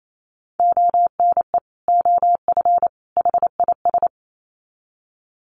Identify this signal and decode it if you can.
Morse code, heard as a keyed tone, about 28 wpm, 700 Hz. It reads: ONE OF HIS
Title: Sets of 3 Words 28wpm